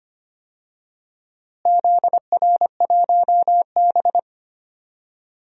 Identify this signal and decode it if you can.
Morse code, heard as a keyed tone, about 25 wpm, 700 Hz. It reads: ZR1B
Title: DXCC-100 Call Signs - 4 Characters - Repeated 2x - 25wpm